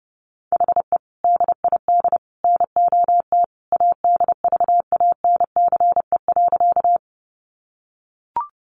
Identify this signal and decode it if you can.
Morse code, heard as a keyed tone, about 30 wpm, 700 Hz. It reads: HE DID NOT ADVANCE.
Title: Sentences from Top 1400 Words 30wpm